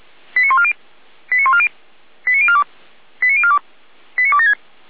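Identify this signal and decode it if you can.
Five-tone selective call (ZVEI), heard as quick runs of five tones, tones of about 70 ms. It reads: 89140 89140 89042 89042 89276
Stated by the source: ZVEI five-tone selective call received on an SDR, logged by Feuerwehr Orlamünde fire brigade.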